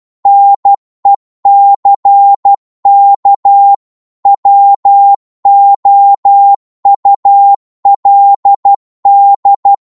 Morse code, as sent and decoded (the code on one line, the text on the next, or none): -. -.-. -.- / .-- --- ..- .-.. -..
NECK WOULD